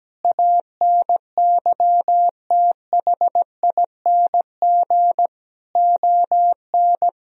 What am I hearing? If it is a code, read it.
Morse code, heard as a keyed tone, about 17 wpm, 700 Hz. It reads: ANYTHING ON
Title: Sentences from Top 1000 Words - Review Part 4 - 17wpm